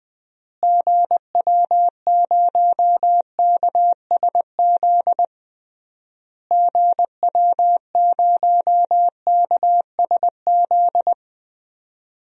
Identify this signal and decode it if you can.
Morse code, heard as a keyed tone, about 20 wpm, 700 Hz. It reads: GW0KSZ GW0KSZ
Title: DXCC-100 Call Signs - 6 Characters - Repeated 3x - 20wpm